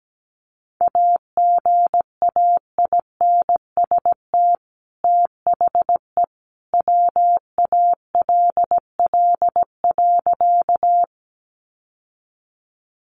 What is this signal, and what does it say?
Morse code, heard as a keyed tone, about 17 wpm, 700 Hz. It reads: AGAINST THE WALL.
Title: Sentences from Top 1100 Words 17wpm